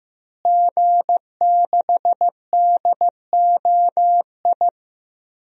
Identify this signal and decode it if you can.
Morse code, heard as a keyed tone, about 15 wpm, 700 Hz. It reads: G6DOI
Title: DXCC-100 Call Signs - 5 Characters - Repeated 3x - 15wpm